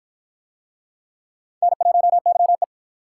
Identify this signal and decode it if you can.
Morse code, heard as a keyed tone, about 51 wpm, 700 Hz. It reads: N1YE